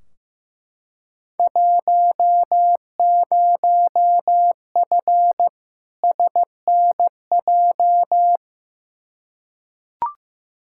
Morse code, heard as a keyed tone, about 15 wpm, 700 Hz. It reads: 10F SNJ
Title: Field Day Exchanges Encore - 15wpm